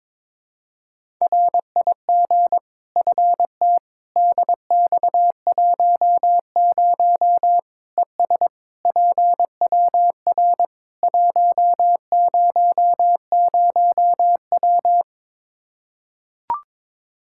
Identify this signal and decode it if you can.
Morse code, heard as a keyed tone, about 21 wpm, 700 Hz. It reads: RIG FT DX10 ES PWR 100W
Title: Practice QSO - Exchange 2 - Info - 22wpm